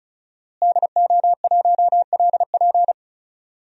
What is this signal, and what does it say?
Morse code, heard as a keyed tone, about 36 wpm, 700 Hz. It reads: DO1LP
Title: DXCC-100 Call Signs - 5 Characters - Repeated 2x - 35wpm